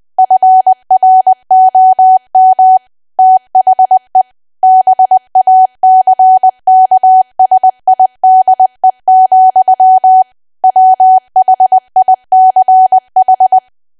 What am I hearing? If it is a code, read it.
Morse code, heard as a keyed tone, about 20 wpm, 750 Hz. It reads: FROM THE BACKSIDE, WHICH
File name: Archive / 20 WPM / 20WPM